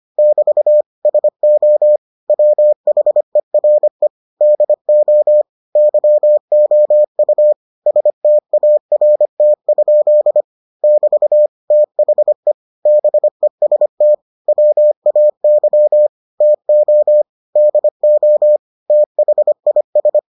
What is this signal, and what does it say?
Morse code, heard as a keyed tone, about 25 wpm, 600 Hz. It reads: <BT> SO WHERE DO YOU START? <BT> THE BEST WAY TO DO THIS